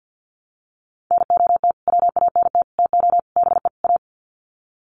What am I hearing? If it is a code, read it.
Morse code, heard as a keyed tone, about 49 wpm, 700 Hz. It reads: NOT WANT TO BE A
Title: Sets of 5 Words Encore 50wpm